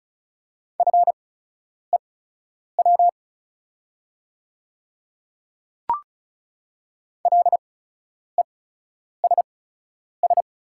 Morse code, heard as a keyed tone, about 36 wpm, spaced as for 10 wpm, 700 Hz. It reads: FEW LESS
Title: 8x Character Spacing - Top 500 Words - 35wpm